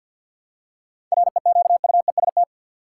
Morse code, heard as a keyed tone, about 50 wpm, 700 Hz. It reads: REQUEST
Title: Seven Letter Words - 50wpm